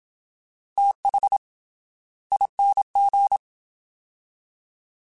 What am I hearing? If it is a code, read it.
Morse code, heard as a keyed tone, about 27 wpm, 800 Hz. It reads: TH ING
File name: TH / TH mp3s / Fast TH @25